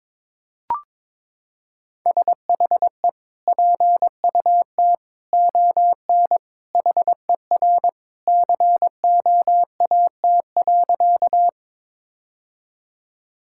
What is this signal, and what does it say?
Morse code, heard as a keyed tone, about 22 wpm, 700 Hz. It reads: SHE PUT ON HER COAT.
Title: Sentences from Top 1600 Words 22wpm